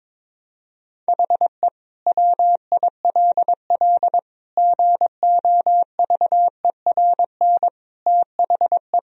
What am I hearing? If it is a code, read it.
Morse code, heard as a keyed tone, about 21 wpm, 700 Hz. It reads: HE WILL GOVERN THE
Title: Sentences from Top 1000 Words - Review Part 2 - 22wpm